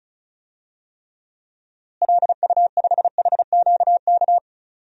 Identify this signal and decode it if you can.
Morse code, heard as a keyed tone, about 36 wpm, 700 Hz. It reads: LU5HQK
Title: DXCC-100 Call Signs - 6 Characters - Repeated 2x - 35wpm